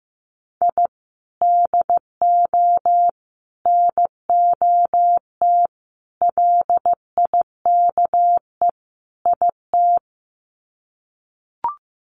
Morse code, heard as a keyed tone, about 15 wpm, 700 Hz. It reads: I DO NOT LIKE IT
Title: Sets of 5 Words Encore 15wpm